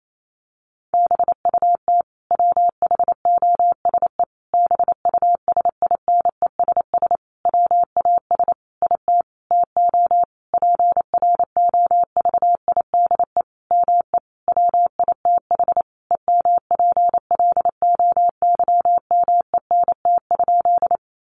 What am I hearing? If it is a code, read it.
Morse code, heard as a keyed tone, about 28 wpm, 700 Hz. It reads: BUT WHOSE BUSINESS WAS IT TO PROVIDE ME WITH EMPLOYMENT?
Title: Sentences from Top 1500 Words 28wpm